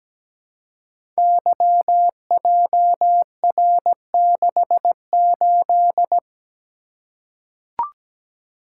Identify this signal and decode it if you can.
Morse code, heard as a keyed tone, about 17 wpm, 700 Hz. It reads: YJR68
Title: Code Groups - 5 Letter:Numbers - 17wpm